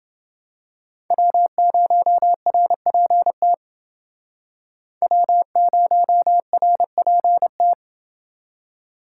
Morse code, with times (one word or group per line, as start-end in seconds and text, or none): 1.1-3.5 W0RPT
5.0-7.7 W0RPT